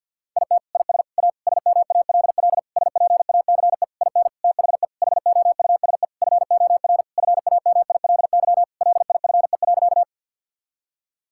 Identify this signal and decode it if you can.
Morse code, heard as a keyed tone, about 50 wpm, 700 Hz. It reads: IT IS A SMALL SPACE IN THE HOUSE FOR FAMILY LIFE.
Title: Sentences from Top 1000 Words - Review Part 5 - 50wpm